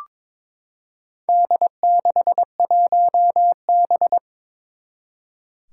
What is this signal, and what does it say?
Morse code, heard as a keyed tone, about 22 wpm, 700 Hz. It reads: D61B
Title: Four Challenging Letter-Number Combinations 22wpm